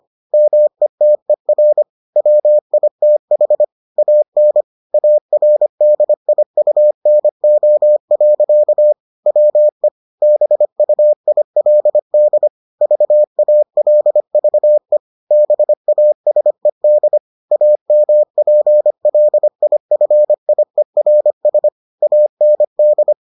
Morse code, heard as a keyed tone, about 25 wpm, 600 Hz. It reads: METER WITH AN ARDIUNO. WE BUILD VALVE BASED AMPLIFIERS AND